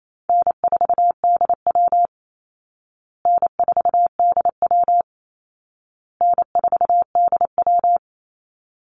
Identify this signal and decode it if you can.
Morse code, heard as a keyed tone, about 28 wpm, 700 Hz. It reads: N4DW N4DW N4DW
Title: Call Signs Repeated 2x - 28wpm